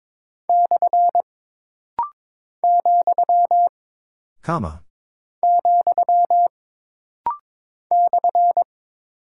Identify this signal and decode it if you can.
Morse code, heard as a keyed tone, about 22 wpm, 700 Hz. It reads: / , , /
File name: practice / 22wpm / Single Punctuation-Prosign - Rapid-Fire - 22wpm